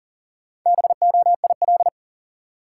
Morse code, heard as a keyed tone, about 40 wpm, 700 Hz. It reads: BOIL
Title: Sets of 5 Words Encore - New Word Drill - 40wpm